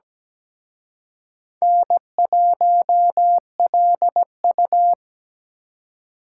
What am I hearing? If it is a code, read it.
Morse code, heard as a keyed tone, about 20 wpm, 700 Hz. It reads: N1LU
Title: Speed Racing - USA 1x2 Call Signs 17wpm